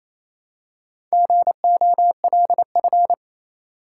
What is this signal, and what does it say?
Morse code, heard as a keyed tone, about 28 wpm, 700 Hz. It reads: GOLF
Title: Four Letter Words - 28wpm